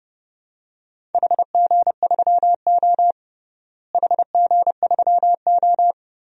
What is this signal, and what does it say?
Morse code, heard as a keyed tone, about 31 wpm, 700 Hz. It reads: HG3O HG3O
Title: DXCC-100 Call Signs - 4 Characters - Repeated 3x - 30wpm